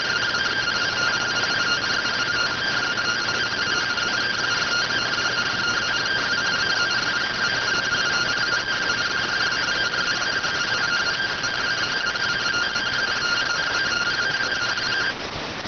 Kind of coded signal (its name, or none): two-tone teleprinter signal